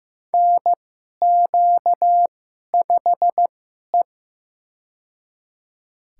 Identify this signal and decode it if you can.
Morse code, heard as a keyed tone, about 15 wpm, spaced as for 11 wpm, 700 Hz. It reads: NQ5E
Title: USA Call Signs - 2x Character Spacing - 15wpm